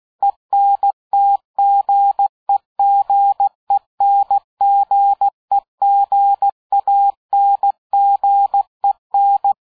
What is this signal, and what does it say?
Morse code, heard as a keyed tone, about 16 wpm, 800 Hz. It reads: ENTGEGENGEGANGEN
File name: dk5ke / Fingerbrecher 75BpM